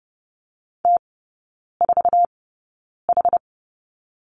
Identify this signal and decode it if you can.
Morse code, heard as a keyed tone, about 30 wpm, spaced as for 9 wpm, 700 Hz. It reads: T4H
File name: 081 Lesson 14e - Number 2 - Three Characters 30wpm